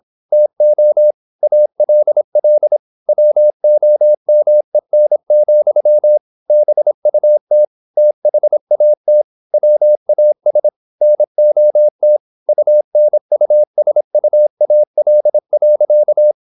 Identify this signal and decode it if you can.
Morse code, heard as a keyed tone, about 26 wpm, 600 Hz. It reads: TO ALL WOMEN, BUT THAT WAS NOT UNUSUAL.
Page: Never 26wpm | CW med Gnister